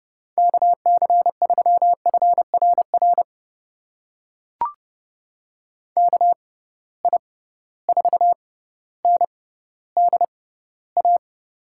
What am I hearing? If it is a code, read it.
Morse code, heard as a keyed tone, about 30 wpm, spaced as for 15 wpm, 700 Hz. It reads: KC3FRR KI4NDA
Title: USA Call Signs - 6 Characters - 6x Character Spacing - 30wpm